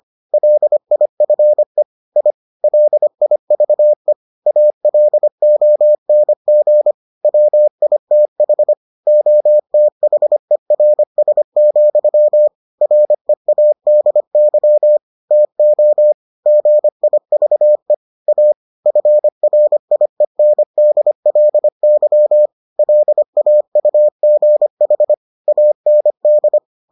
Morse code, heard as a keyed tone, about 25 wpm, 600 Hz. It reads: LIFE I LIVE ALONG WITH OTHERS, READY TO GIVE A FRIENDLY LAUGH AND